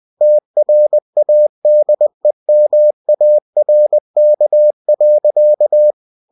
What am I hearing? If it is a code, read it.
Morse code, heard as a keyed tone, about 20 wpm, 600 Hz. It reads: TRADEMARK.
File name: cw-books / winnie-the-pooh / winnie-20wpm-page-0113